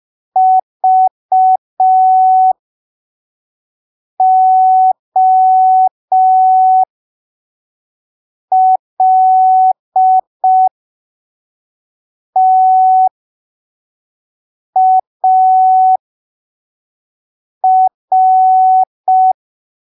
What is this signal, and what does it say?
Morse code, heard as a keyed tone, about 5 wpm, spaced as for 3 wpm, 750 Hz. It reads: VOLTAR
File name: arquivos / texto / texto 15 05ppm